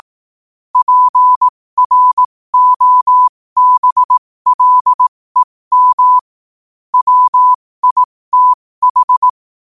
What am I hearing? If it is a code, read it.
Morse code, heard as a keyed tone, about 18 wpm, spaced as for 15 wpm, 1000 Hz. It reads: PROBLEM WITH
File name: qotdfiles / 15wpm